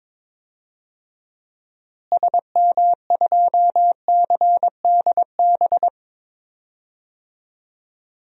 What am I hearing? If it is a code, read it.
Morse code, heard as a keyed tone, about 22 wpm, 700 Hz. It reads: SM2CDB
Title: DXCC-100 Call Signs - 6 Characters - Repeated 2x - 22wpm